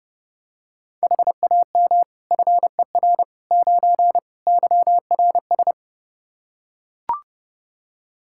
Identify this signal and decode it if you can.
Morse code, heard as a keyed tone, about 32 wpm, 700 Hz. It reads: HAM FER 9 YRS
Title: K1USN SST QSO Practice 30wpm